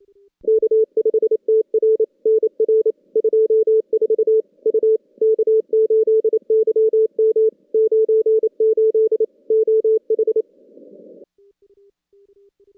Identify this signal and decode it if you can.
Morse code, heard as a keyed tone, about 28 wpm, 440 Hz. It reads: K5TR NR 24 U K8YM 98 OH
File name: contests / CW-SS-k5tr- -2200